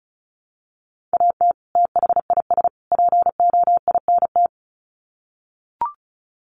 Morse code, heard as a keyed tone, about 34 wpm, 700 Hz. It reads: AT THIS POINT
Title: Sets of 3 Words 35wpm